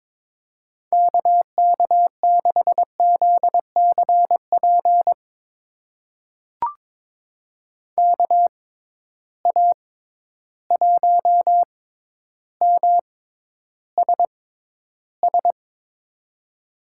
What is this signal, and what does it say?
Morse code, heard as a keyed tone, about 21 wpm, spaced as for 11 wpm, 700 Hz. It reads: KK6ZCP KA1MSS
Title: USA Call Signs - 6 Characters - 6x Character Spacing - 22wpm